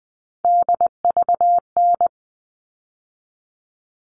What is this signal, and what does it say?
Morse code, heard as a keyed tone, about 20 wpm, 700 Hz. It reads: DVN
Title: All Cut Numbers - 3-Digit 20wpm